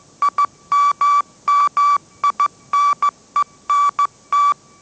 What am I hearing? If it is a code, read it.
Morse code, heard as a keyed tone, about 15 wpm, 1150 Hz. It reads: IMMINENT